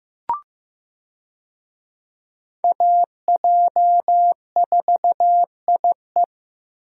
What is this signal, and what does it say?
Morse code, heard as a keyed tone, about 15 wpm, 700 Hz. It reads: AJ4IE
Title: USA Call Signs - 5 Characters - Repeated 3x - 15wpm